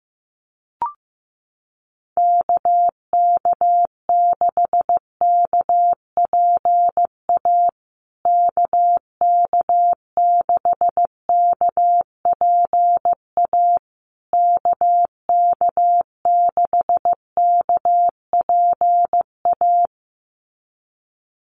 Morse code, heard as a keyed tone, about 15 wpm, 700 Hz. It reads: KK6KPA KK6KPA KK6KPA
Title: USA Call Signs Repeated 3x 15wpm